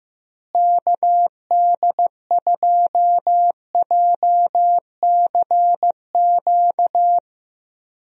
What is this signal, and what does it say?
Morse code, heard as a keyed tone, about 15 wpm, 700 Hz. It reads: KD2JCQ
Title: USA Call Signs - 6 Characters - Repeated 3x - 15wpm